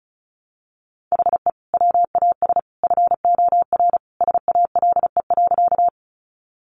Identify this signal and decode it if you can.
Morse code, heard as a keyed tone, about 34 wpm, 700 Hz. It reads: HE WAS FOR SALE.
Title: Sentences from Top 1400 Words 35wpm